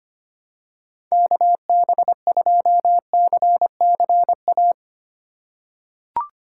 Morse code, heard as a keyed tone, about 25 wpm, 700 Hz. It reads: KB2CCA
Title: USA Call Signs - 6 Characters - 2x Character Spacing - 25wpm